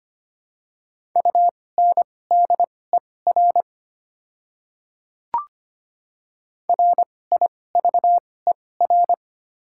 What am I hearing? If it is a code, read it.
Morse code, heard as a keyed tone, about 25 wpm, spaced as for 18 wpm, 700 Hz. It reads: UNDER RIVER
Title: 2x Character Spacing - Top 500 Words - 25wpm